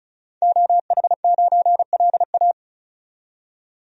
Morse code, heard as a keyed tone, about 34 wpm, 700 Hz. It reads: OH9LA
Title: DXCC-100 Call Signs - 5 Characters - 35wpm